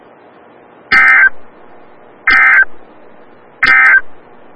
Emergency Alert System (SAME) tones, heard as harsh buzzing bursts. It is heard from 0.8 to 4.1 s.
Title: Des Moines, IA NOAA Weather Radio - WXL57